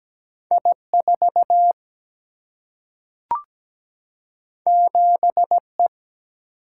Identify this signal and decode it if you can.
Morse code, heard as a keyed tone, about 17 wpm, 700 Hz. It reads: I4 7E
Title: Single-Letter Single-Number 17wpm